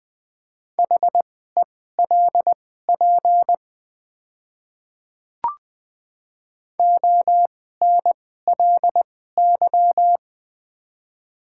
Morse code, heard as a keyed tone, about 20 wpm, spaced as for 14 wpm, 700 Hz. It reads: HELP ONLY
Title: 2x Character Spacing - Top 200 Words - 20wpm